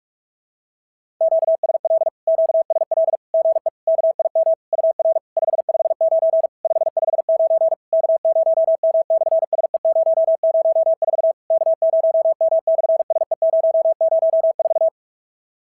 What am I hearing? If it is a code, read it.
Morse code, heard as a keyed tone, about 46 wpm, 650 Hz. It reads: QSL QSL GE KIM UR 559 559 K0M/SE004 K0M/SE004
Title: SOTA QSO Practice 45wpm